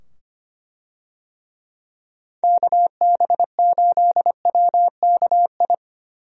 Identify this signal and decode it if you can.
Morse code, heard as a keyed tone, about 25 wpm, 700 Hz. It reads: KB8WKI
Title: USA Call Signs - 6 Characters - 25wpm